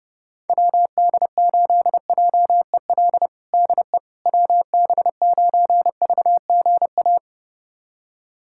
Morse code, heard as a keyed tone, about 30 wpm, 700 Hz. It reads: WD8JEL DE WB9VGA
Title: USA Callsign DE Callsign - 6 Characters - 30wpm